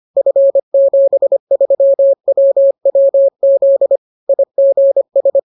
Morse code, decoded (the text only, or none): F73WWZ IGS